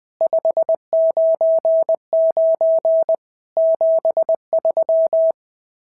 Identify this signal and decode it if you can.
Morse code, heard as a keyed tone, about 20 wpm, 650 Hz. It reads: 599 73
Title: SOTA QSO Practice 20wpm